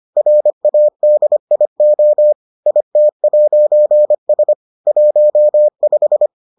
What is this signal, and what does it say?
Morse code, heard as a keyed tone, about 25 wpm, 600 Hz. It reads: RADIO IT'S 15